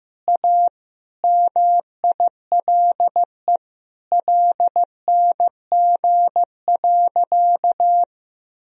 Morse code, heard as a keyed tone, about 15 wpm, 700 Hz. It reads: A MILE LNG.
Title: Sentences from Top 500 Words and Abbreviations 15wpm